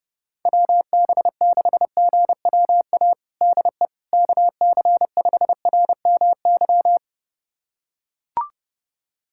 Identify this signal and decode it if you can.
Morse code, heard as a keyed tone, about 30 wpm, 700 Hz. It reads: WB6GWA DE KC5RMY
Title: USA Callsign DE Callsign 30wpm